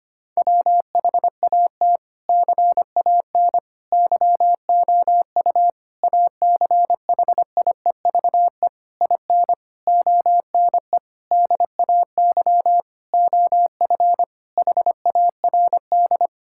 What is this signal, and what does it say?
Morse code, heard as a keyed tone, about 25 wpm, 700 Hz. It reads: WHAT CAN YOU ACHIEVE IN ONE DAY OF HARD
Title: Sentences from Top 1000 Words - Review Part 6 - 25wpm